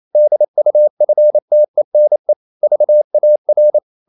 Morse code, heard as a keyed tone, about 28 wpm, 600 Hz. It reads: DUFTENE VAR
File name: Ferie QRQ 28wpm